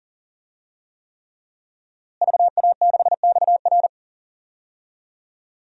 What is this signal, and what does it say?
Morse code, heard as a keyed tone, about 40 wpm, 700 Hz. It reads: VA6XR
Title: DXCC-100 Call Signs - 5 Characters - Repeated 3x - 40wpm